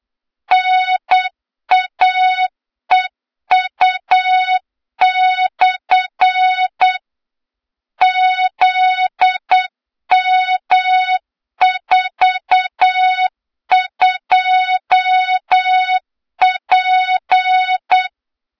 Morse code, heard as a keyed tone, about 8 wpm, 3700 Hz, with noise. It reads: NAEU/ ZM42P
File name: Morze-HA2MN / hang / 23 GYAKORLAT